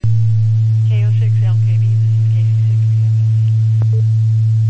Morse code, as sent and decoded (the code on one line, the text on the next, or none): none